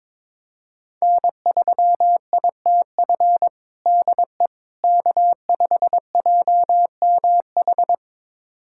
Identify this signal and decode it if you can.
Morse code, heard as a keyed tone, about 22 wpm, 700 Hz. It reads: N3ITF DE K5JMH
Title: USA Callsign DE Callsign - 5 Characters - 22wpm